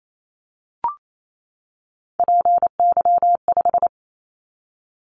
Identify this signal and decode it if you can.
Morse code, heard as a keyed tone, about 28 wpm, 700 Hz. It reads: PY5
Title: Callsign Prefixes for Top 100 DXCC - 28wpm